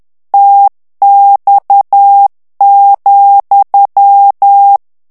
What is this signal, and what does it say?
Morse code, heard as a keyed tone, about 10 wpm, 800 Hz. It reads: TX,
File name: ITALIEN texte clair 2 vitesse 10 mots minute